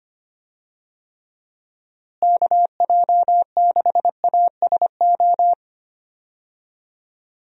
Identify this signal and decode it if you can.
Morse code, heard as a keyed tone, about 25 wpm, 700 Hz. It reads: KJ6ASO
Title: USA Call Signs - 6 Characters - Repeated 3x - 25wpm